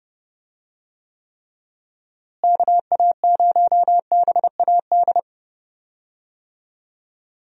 Morse code, heard as a keyed tone, about 30 wpm, 700 Hz. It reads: KA0BAD
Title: USA Call Signs - 6 Characters - Repeated 2x - 30wpm